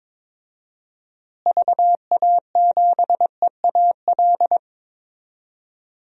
Morse code, heard as a keyed tone, about 22 wpm, 700 Hz. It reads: VA7EAL
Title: DXCC-100 Call Signs - 6 Characters - Repeated 3x - 22wpm